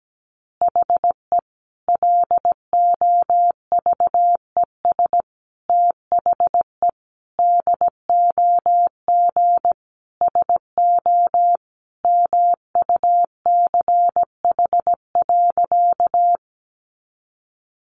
Morse code, heard as a keyed tone, about 17 wpm, 700 Hz. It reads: HE LOVES THE DOG SO MUCH.
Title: Additional Easy Phrases 17wpm